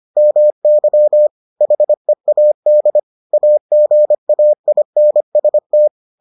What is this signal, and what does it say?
Morse code, heard as a keyed tone, about 25 wpm, 600 Hz. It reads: MY HEAD AGAINST